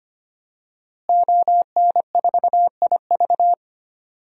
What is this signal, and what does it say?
Morse code, heard as a keyed tone, about 25 wpm, 700 Hz. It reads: ON4IV
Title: DXCC-100 Call Signs - 5 Characters - Repeated 3x - 25wpm